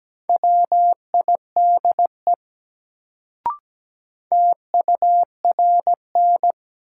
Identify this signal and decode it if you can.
Morse code, heard as a keyed tone, about 17 wpm, 700 Hz. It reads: WIDE TURN
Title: Sets of 4 Words - New Word Drill - 17wpm